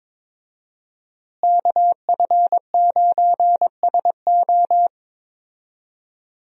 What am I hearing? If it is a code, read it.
Morse code, heard as a keyed tone, about 22 wpm, 700 Hz. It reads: KF9SO
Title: USA Call Signs - 5 Characters - Repeated 3x - 22wpm